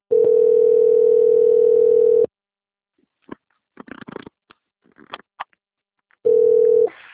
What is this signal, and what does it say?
Ringback tone (North America), 440 + 480 Hz.